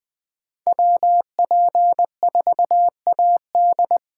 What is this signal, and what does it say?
Morse code, heard as a keyed tone, about 21 wpm, 700 Hz. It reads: WP4AD